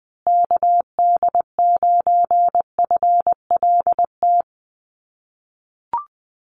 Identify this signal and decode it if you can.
Morse code, heard as a keyed tone, about 20 wpm, 700 Hz. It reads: KD9FLT
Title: USA Call Signs Repeated 3x 20wpm